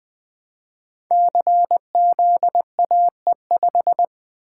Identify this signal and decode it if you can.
Morse code, heard as a keyed tone, about 20 wpm, 700 Hz. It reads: CZAE5
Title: Code Groups - 5 Letter:Numbers - 20wpm